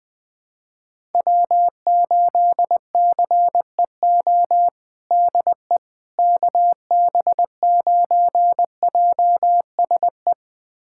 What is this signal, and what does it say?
Morse code, heard as a keyed tone, about 21 wpm, 700 Hz. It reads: W8CEO DE KB9JSE
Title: USA Callsign DE Callsign 20wpm